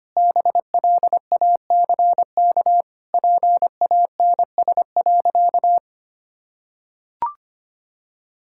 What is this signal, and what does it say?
Morse code, heard as a keyed tone, about 25 wpm, 700 Hz. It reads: BLACK PANS.
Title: Pangram Challenge v2 - 25wpm